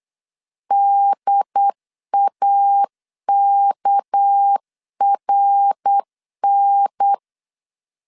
Morse code, heard as a keyed tone, about 8 wpm, 800 Hz. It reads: DAKRN
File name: CW / DET / 1 9 1500C S C 8wpm 5653